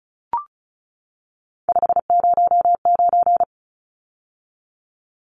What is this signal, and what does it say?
Morse code, heard as a keyed tone, about 34 wpm, 700 Hz. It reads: 509
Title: Numbers - 3-Digit 35wpm